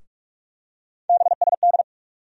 Morse code, heard as a keyed tone, about 44 wpm, 700 Hz. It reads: BSD
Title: USA Call Sign Postfixes - 3 Letters 45wpm